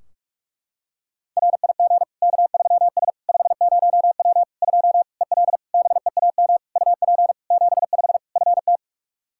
Morse code, heard as a keyed tone, about 46 wpm, 700 Hz. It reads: RIG K3S 50W 2 EL BEAM UP 75 FT
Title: Practice QSO - Exchange 2 - Info - 45wpm